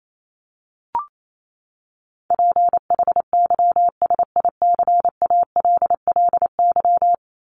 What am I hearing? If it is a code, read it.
Morse code, heard as a keyed tone, about 28 wpm, 700 Hz. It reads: PHYSICALLY
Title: Top 1501-1750 Words 28wpm